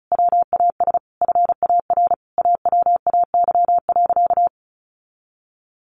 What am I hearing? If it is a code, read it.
Morse code, heard as a keyed tone, about 34 wpm, 700 Hz. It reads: WAS FAR AWAY.
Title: Sentences from Top 1500 Words 35wpm